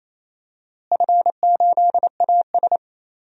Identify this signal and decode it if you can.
Morse code, heard as a keyed tone, about 28 wpm, 700 Hz. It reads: F8AS
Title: DXCC-100 Call Signs - 4 Characters - Repeated 3x - 28wpm